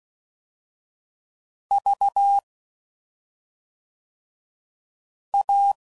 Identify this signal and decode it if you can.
Morse code, heard as a keyed tone, about 16 wpm, 800 Hz. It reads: V A
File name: Suffixes / Special ING's / SlowToFaster ING's